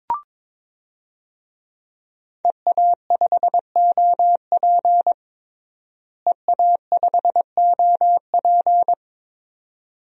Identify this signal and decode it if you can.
Morse code, heard as a keyed tone, about 22 wpm, 700 Hz. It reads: EA5OP EA5OP
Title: DXCC-100 Call Signs - 5 Characters - Repeated 2x - 22wpm